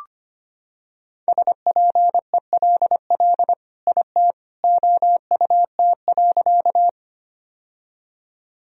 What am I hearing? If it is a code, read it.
Morse code, heard as a keyed tone, about 25 wpm, 700 Hz. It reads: SPELL IT OUT.
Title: Sentences from Top 1000 Words - Review Part 1 - 25wpm